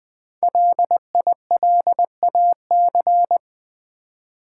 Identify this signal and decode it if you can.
Morse code, heard as a keyed tone, about 20 wpm, 700 Hz. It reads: LILAC
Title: Easy Words with Hints - 20wpm